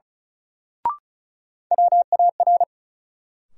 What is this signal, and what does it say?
Morse code, heard as a keyed tone, about 36 wpm, 700 Hz. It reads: WAR
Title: Sets of 5 Words - New Word Drill - 35wpm